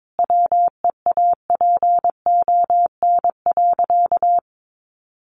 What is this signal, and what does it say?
Morse code, heard as a keyed tone, about 22 wpm, 700 Hz. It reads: WEAPON.
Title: A Sound Of Thunder 22wpm